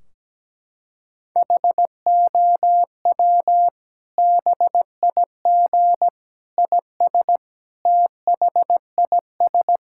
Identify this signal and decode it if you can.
Morse code, heard as a keyed tone, about 17 wpm, 700 Hz. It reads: HOW BIG IS THIS
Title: Sentences from Top 1600 Words 17wpm